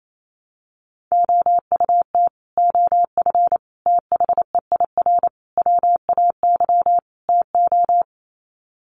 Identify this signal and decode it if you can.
Morse code, heard as a keyed tone, about 28 wpm, 700 Hz. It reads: OUT OF THEIR WAY TO
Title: Sets of 5 Words Encore2 28wpm